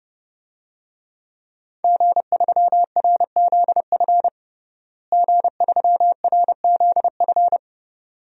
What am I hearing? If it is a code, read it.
Morse code, heard as a keyed tone, about 30 wpm, 700 Hz. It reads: G3RZF G3RZF
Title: DXCC-100 Call Signs - 5 Characters - Repeated 2x - 30wpm